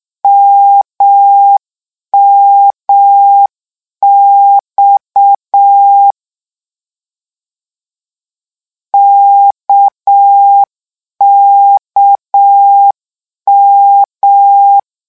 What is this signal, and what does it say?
Morse code, heard as a keyed tone, about 6 wpm, 800 Hz. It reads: MMX KKM
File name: lecon 1-vitesse 6